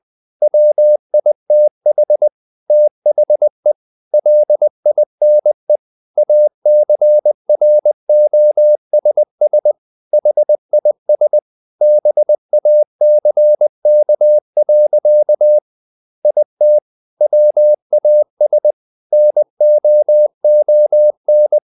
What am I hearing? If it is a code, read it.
Morse code, heard as a keyed tone, about 20 wpm, 600 Hz. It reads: WITH THE LINE ACROSS HIS BACK. IT WAS NOON
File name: the old man and the sea / the old man and the sea part 1 20wpm